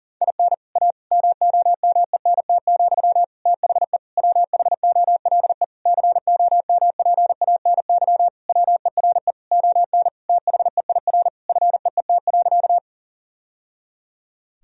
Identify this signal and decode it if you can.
Morse code, heard as a keyed tone, about 40 wpm, 700 Hz. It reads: IN A MOMENT, THE WHOLE COMPANY WERE ON THEIR FEET.
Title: Sentences from Top 1000 Words - Review Part 3 - 40wpm